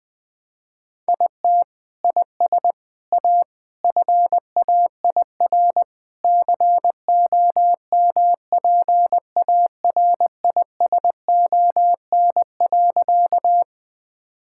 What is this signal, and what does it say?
Morse code, heard as a keyed tone, about 19 wpm, 700 Hz. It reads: IT IS A FAIR COMPARISON.
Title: Sentences from Top 1600 Words 20wpm